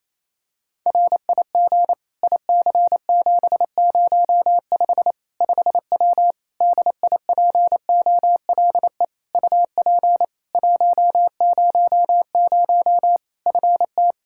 Morse code, heard as a keyed tone, about 28 wpm, 700 Hz. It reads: RIG IC705 5W DIPOLE UP 100 FT
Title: K1USN SST QSO Practice 28wpm